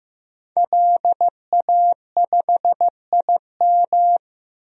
Morse code, heard as a keyed tone, about 15 wpm, 700 Hz. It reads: LA5IM
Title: DXCC-100 Call Signs - 5 Characters - 15wpm